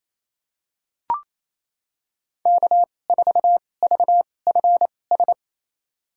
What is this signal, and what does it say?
Morse code, heard as a keyed tone, about 28 wpm, spaced as for 20 wpm, 700 Hz. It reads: K4VFS
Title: USA Call Signs - 5 Characters - 2x Character Spacing - 28wpm